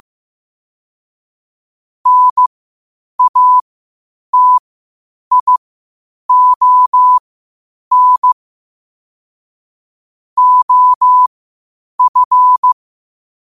Morse code, heard as a keyed tone, about 15 wpm, spaced as for 8 wpm, 1000 Hz. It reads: NATION OF
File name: qotdfiles / 8wpm